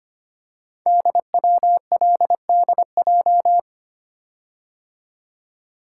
Morse code, heard as a keyed tone, about 25 wpm, 700 Hz. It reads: DWLDJ
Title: Code Groups - 5 Letters - 25wpm